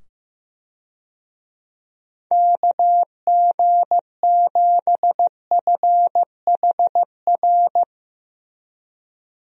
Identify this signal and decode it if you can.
Morse code, heard as a keyed tone, about 15 wpm, 700 Hz. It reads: KG7FHR
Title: USA Call Signs - 6 Characters - 15wpm